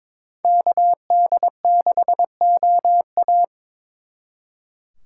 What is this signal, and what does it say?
Morse code, heard as a keyed tone, about 27 wpm, 700 Hz. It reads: KD6OA
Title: Speed Racing - USA 2x2 Call Signs 22wpm